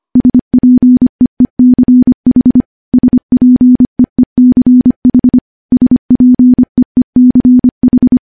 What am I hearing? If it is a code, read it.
Morse code, heard as a keyed tone, about 25 wpm, 250 Hz. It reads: SPEECH SPEECH SPEECH
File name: morse / morse-training-25WPM